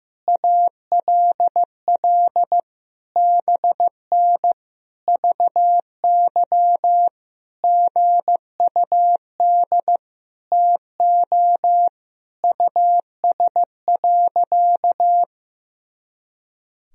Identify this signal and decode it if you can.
Morse code, heard as a keyed tone, about 15 wpm, 700 Hz. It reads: ALL BN VY GUD TO US.
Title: Sentences from Top 200 Words and Abbreviations 15wpm